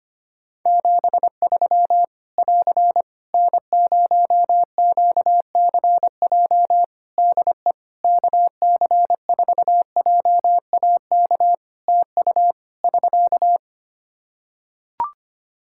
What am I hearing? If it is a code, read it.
Morse code, heard as a keyed tone, about 25 wpm, 700 Hz. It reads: 73 <AR> N0QCJ DE KC4JAK TU <SK>
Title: K1USN SST QSO Practice 25wpm